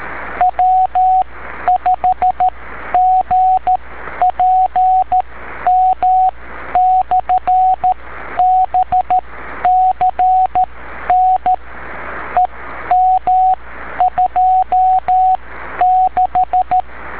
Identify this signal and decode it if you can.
Morse code, heard as a keyed tone, about 13 wpm, spaced as for 10 wpm, 740 Hz, with radio noise. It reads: W5GPM/BCN EM26
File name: beacons / w5gpm-b